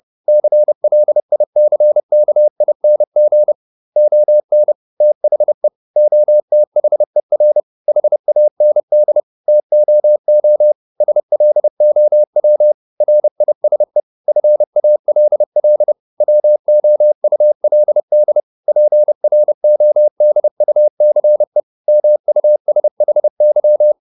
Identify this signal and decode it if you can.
Morse code, heard as a keyed tone, about 30 wpm, 600 Hz. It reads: CLICKING ON THE OTHER HAND TOO SLOW RISE FALL WOULD PRODUCE MUSHY